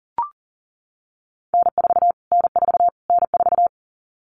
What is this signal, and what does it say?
Morse code, heard as a keyed tone, about 40 wpm, 700 Hz. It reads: N4 N4 N4
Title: Callsign Prefixes for Top 25 DXCC - Repeated 3x - 40wpm